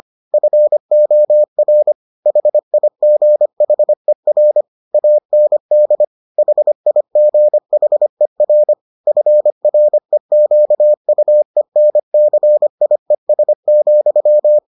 Morse code, heard as a keyed tone, about 25 wpm, 600 Hz. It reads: FOR HIGHER AND HIGHER FREQUENCIES,